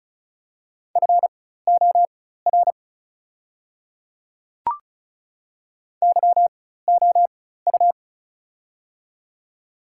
Morse code, heard as a keyed tone, about 36 wpm, spaced as for 16 wpm, 700 Hz. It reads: FOR YOU
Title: 4x Character Spacing - Top 100 Words - 35wpm